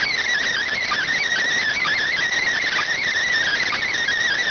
Emergency Alert System (SAME) tones, heard as harsh buzzing bursts. It is heard from 0.5 to 4.0 s.